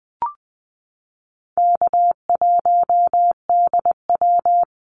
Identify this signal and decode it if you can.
Morse code, heard as a keyed tone, about 20 wpm, 700 Hz. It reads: K1DW
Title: Easy Call Signs - 20wpm